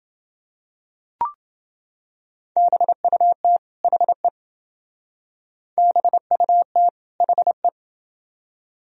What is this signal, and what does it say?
Morse code, heard as a keyed tone, about 28 wpm, 700 Hz. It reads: BUT HE BUT HE